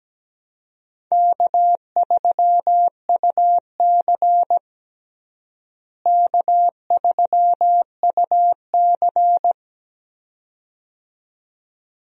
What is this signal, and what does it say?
Morse code, heard as a keyed tone, about 17 wpm, 700 Hz. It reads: K3UC K3UC
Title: USA Call Signs - 4 Characters - Repeated 2x - 17wpm